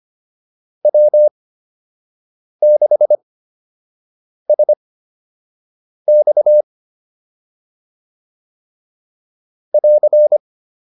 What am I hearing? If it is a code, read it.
Morse code, heard as a keyed tone, about 25 wpm, spaced as for 6 wpm, 600 Hz. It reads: W6SX <AR>
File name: QSO 102 06